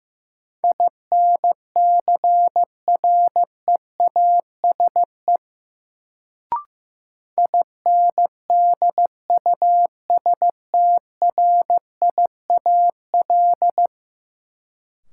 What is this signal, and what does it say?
Morse code, heard as a keyed tone, about 15 wpm, 700 Hz. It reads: INCREASE INDUSTRIAL
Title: Sets of 5 Words - New Word Drill - 15wpm